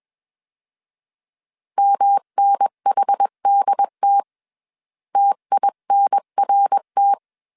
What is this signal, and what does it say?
Morse code, heard as a keyed tone, about 21 wpm, 800 Hz. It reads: MNHDT TINRT